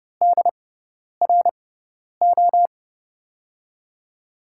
Morse code, heard as a keyed tone, about 30 wpm, spaced as for 10 wpm, 700 Hz. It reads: DRO
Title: USA Call Signs - 6 Characters - 6x Character Spacing - 30wpm